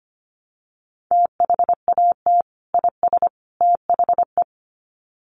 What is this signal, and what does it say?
Morse code, heard as a keyed tone, about 25 wpm, 700 Hz. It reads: THAT IS THE
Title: Sets of 3 Words 25wpm